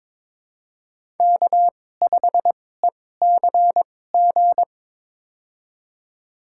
Morse code, heard as a keyed tone, about 22 wpm, spaced as for 16 wpm, 700 Hz. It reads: K5ECG
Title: USA Call Signs - 2x Character Spacing - 22wpm